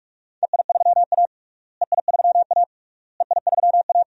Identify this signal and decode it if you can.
Morse code, heard as a keyed tone, about 42 wpm, 700 Hz. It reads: EI3A EI3A EI3A
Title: DXCC-100 Call Signs - 4 Characters - Repeated 3x - 45wpm